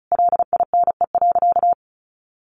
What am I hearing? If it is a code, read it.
Morse code, heard as a keyed tone, about 34 wpm, 700 Hz. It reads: LINE.
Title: Additional Easy Phrases 35wpm